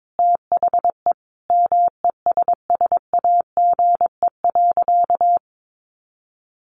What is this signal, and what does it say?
Morse code, heard as a keyed tone, about 22 wpm, 700 Hz. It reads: THE MESSAGE.
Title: Sentences from Top 1100 Words 22wpm